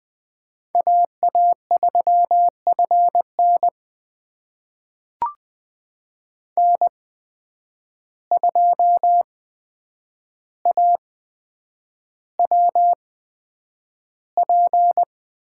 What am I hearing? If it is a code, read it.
Morse code, heard as a keyed tone, about 20 wpm, spaced as for 9 wpm, 700 Hz. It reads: AA3FN N2AWP
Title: USA Call Signs - 5 Characters - 8x Character Spacing - 20wpm